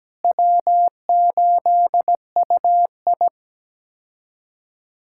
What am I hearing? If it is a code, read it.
Morse code, heard as a keyed tone, about 17 wpm, 700 Hz. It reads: W8UI
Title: K1USN SST QSO Practice - 17wpm